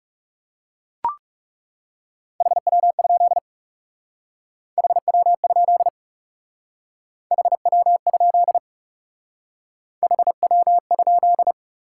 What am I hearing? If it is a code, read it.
Morse code, heard as a keyed tone, about 38 wpm, 700 Hz. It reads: HW? HW? HW? HW?